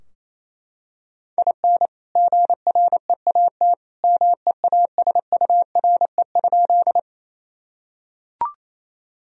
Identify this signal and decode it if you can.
Morse code, heard as a keyed tone, about 28 wpm, 700 Hz. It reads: IN GREAT MEASURE?
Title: Sentences from Top 1000 Words - Review Part 2 - 28wpm